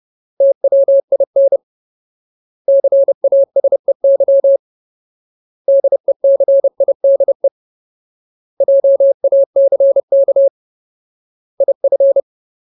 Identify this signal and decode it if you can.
Morse code, heard as a keyed tone, about 30 wpm, 550 Hz. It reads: TWIN CASEY DECIDE JACK IF
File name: Ultimate Word List / 30WPM Ultimate